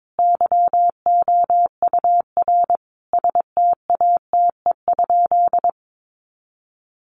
Morse code, heard as a keyed tone, about 22 wpm, 700 Hz. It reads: YOUR STATE?
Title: Sentences from Top 1200 Words 22wpm